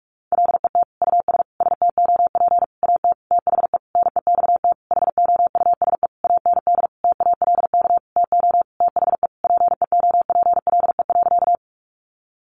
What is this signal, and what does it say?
Morse code, heard as a keyed tone, about 44 wpm, 700 Hz. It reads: LET US STOP AT THE NEXT HOUSE AND TALK TO THE PEOPLE.
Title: Sentences from Top 400 Words 45wpm